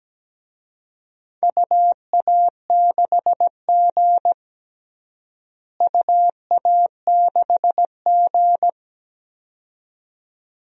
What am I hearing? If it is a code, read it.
Morse code, heard as a keyed tone, about 17 wpm, 700 Hz. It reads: UA6G UA6G
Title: DXCC-100 Call Signs - 4 Characters - Repeated 2x - 17wpm